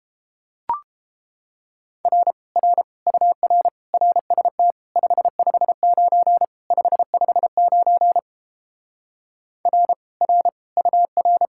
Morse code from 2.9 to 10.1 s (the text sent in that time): UR RST 559 559 R